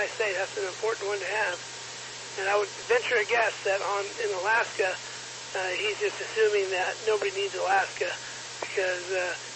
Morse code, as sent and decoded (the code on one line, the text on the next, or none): none